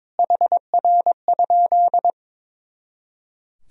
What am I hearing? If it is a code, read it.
Morse code, heard as a keyed tone, about 22 wpm, 700 Hz. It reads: HR?
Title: Sentences from Top 200 Words and Abbreviations 22wpm